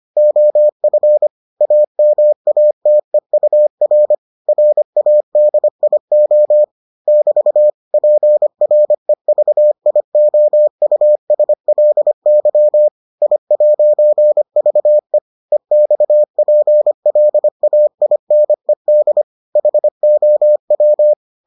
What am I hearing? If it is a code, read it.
Morse code, heard as a keyed tone, about 25 wpm, 600 Hz. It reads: OF AMATEUR RADIO <BT> PREVIOUSLY I'VE EXPLAINED HOW